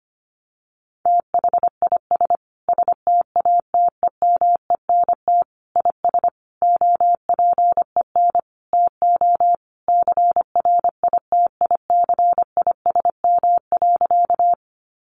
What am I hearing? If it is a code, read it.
Morse code, heard as a keyed tone, about 25 wpm, 700 Hz. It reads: THIS STATEMENT IS OPEN TO CRITICISM.
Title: Sentences from Top 1300 Words 25wpm